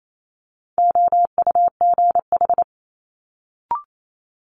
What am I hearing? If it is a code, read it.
Morse code, heard as a keyed tone, about 28 wpm, 700 Hz. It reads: OUGH
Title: Multi-letter Phonemes Quick Fire 28wpm